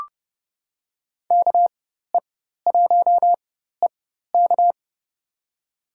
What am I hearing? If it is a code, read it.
Morse code, heard as a keyed tone, about 30 wpm, spaced as for 14 wpm, 700 Hz. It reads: KE1EK
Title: USA Call Signs - 5 Characters - 4x Character Spacing - 30wpm